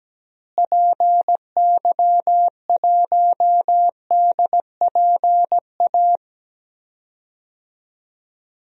Morse code, heard as a keyed tone, about 17 wpm, 700 Hz. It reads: PY1DPA
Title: DXCC-100 Call Signs - 6 Characters - Repeated 2x - 17wpm